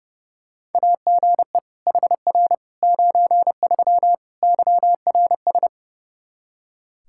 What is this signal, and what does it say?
Morse code, heard as a keyed tone, about 30 wpm, 700 Hz. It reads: AGE HR 93 YRS
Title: Practice QSO - Exchange 3 - More Info - 30wpm